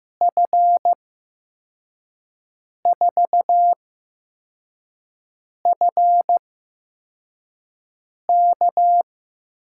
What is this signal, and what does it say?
Morse code, heard as a keyed tone, about 15 wpm, 700 Hz. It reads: F 4 F K
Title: USA Call Signs - 8x Character Spacing - 15wpm